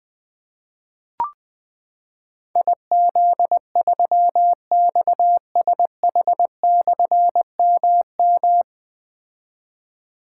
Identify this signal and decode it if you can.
Morse code, heard as a keyed tone, about 20 wpm, 700 Hz. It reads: IZ3XSH/MM
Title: Difficult Call Signs 20wpm